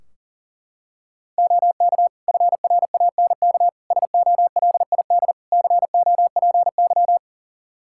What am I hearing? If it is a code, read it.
Morse code, heard as a keyed tone, about 40 wpm, 700 Hz. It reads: OK FRANK SOLID COPY
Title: Practice QSO - Exchange 3 - More Info - 40wpm